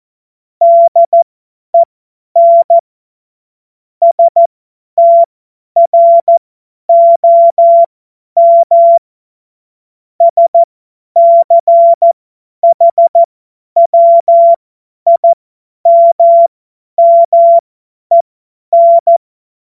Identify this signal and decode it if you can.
Morse code, heard as a keyed tone, about 14 wpm, spaced as for 10 wpm, 680 Hz. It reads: DEN STROM SCHWIMMEN
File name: Morsekurs / Witze / tz 08. November 2022 50 F70